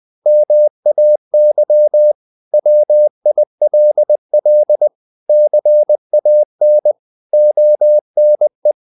Morse code, heard as a keyed tone, about 20 wpm, 600 Hz. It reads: MAY WILL CAN ONE
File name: CW / Practice / 500MostCommonEnglishWords20